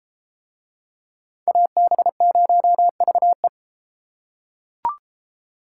Morse code, heard as a keyed tone, about 33 wpm, 700 Hz. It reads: AB0VE